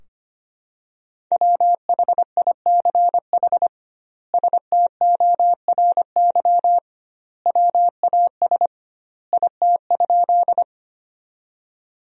Morse code, heard as a keyed tone, about 25 wpm, 700 Hz. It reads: WHICH STORY WAS IT?
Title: 2x Word-Spacing - Sentences from Top 300 Words - 25wpm